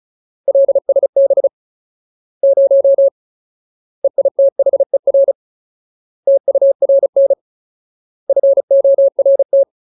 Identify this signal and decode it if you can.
Morse code, heard as a keyed tone, about 34 wpm, 550 Hz. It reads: LSB 0 EITHER TURN FORT
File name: Ultimate Word List / 35WPM Ultimate